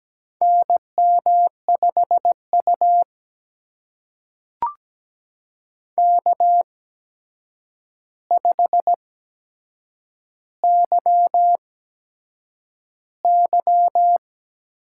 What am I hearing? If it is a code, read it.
Morse code, heard as a keyed tone, about 17 wpm, spaced as for 7 wpm, 700 Hz. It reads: NM5U K5YY
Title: USA Call Signs - 4 Characters - 8x Character Spacing - 17wpm